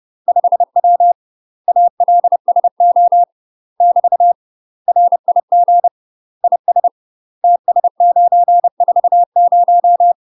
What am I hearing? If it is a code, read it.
Morse code, heard as a keyed tone, about 30 wpm, 700 Hz. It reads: 5W ALSO <BT> RIG IS TS940
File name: UK MASTER QSOs TRAINING / 30WPM / UK MASTER QSOs 0030